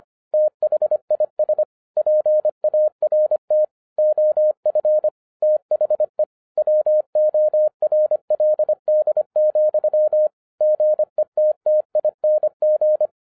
Morse code, heard as a keyed tone, about 25 wpm, 610 Hz. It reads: THIS PART OF THE WORLD, GETTING
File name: 2012-01-mp3 / n14-FUN WITH NUMBERS-25wpm